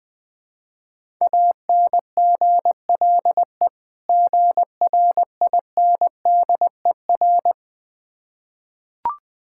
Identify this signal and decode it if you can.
Morse code, heard as a keyed tone, about 20 wpm, 700 Hz. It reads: ANGLE GRINDER
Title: Difficult Words with Hints - 20wpm